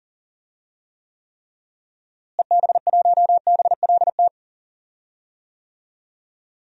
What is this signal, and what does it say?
Morse code, heard as a keyed tone, about 40 wpm, 700 Hz. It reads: EB1BLT
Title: DXCC-100 Call Signs - 6 Characters - Repeated 3x - 40wpm